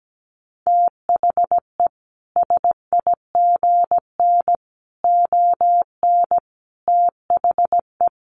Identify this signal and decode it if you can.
Morse code, heard as a keyed tone, about 17 wpm, 700 Hz. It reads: THE SIGN ON THE
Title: A Sound Of Thunder 17wpm